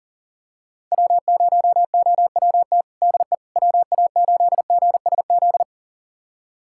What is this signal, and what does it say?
Morse code, heard as a keyed tone, about 40 wpm, 700 Hz. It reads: W0OWT DE WA8GSZ
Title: USA Callsign DE Callsign 40wpm